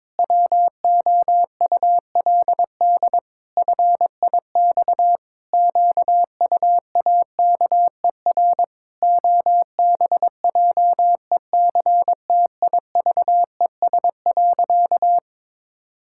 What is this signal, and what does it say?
Morse code, heard as a keyed tone, about 22 wpm, 700 Hz. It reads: WOULD FIX QUAKER OBJECTIVES.
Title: Pangram Challenge v2 - 22wpm